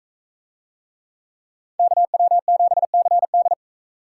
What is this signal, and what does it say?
Morse code, heard as a keyed tone, about 38 wpm, 700 Hz. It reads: KW7CD